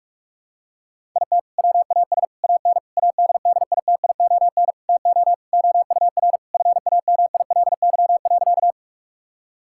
Morse code, heard as a keyed tone, about 46 wpm, 700 Hz. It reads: IT WAS AN ADDITION TO OUR FAMILY.
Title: Sentences from Top 1000 Words - Review Part 6 - 45wpm